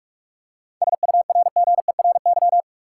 Morse code, heard as a keyed tone, about 41 wpm, 700 Hz. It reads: SURGERY
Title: Seven Letter Words - 45wpm